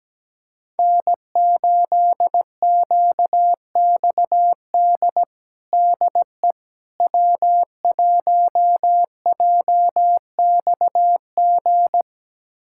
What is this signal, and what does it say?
Morse code, heard as a keyed tone, about 17 wpm, 700 Hz. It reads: N8QXD DE W1JXG
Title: USA Callsign DE Callsign 17wpm